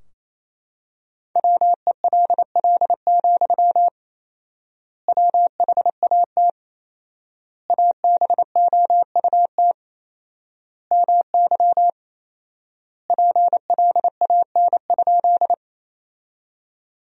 Morse code, heard as a keyed tone, about 28 wpm, 700 Hz. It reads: WELL, WHAT ABOUT MY PLAN?
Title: 4x Word-Spacing - Sentences from Top 500 Words - 28wpm